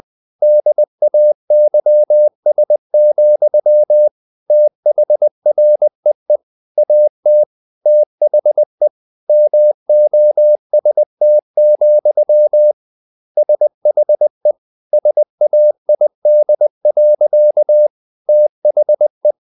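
Morse code, heard as a keyed tone, about 20 wpm, 600 Hz. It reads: DAYS, THREE AT THE MOST, SHE SAID. THE
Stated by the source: Morse code at 19 wpm from CW med Gnister, reading Ken Follett's novel Never in English.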